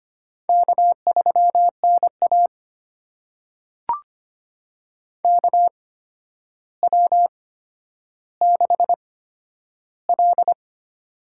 Morse code, heard as a keyed tone, about 25 wpm, spaced as for 11 wpm, 700 Hz. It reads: K3NA KW6L
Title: USA Call Signs - 4 Characters - 8x Character Spacing - 25wpm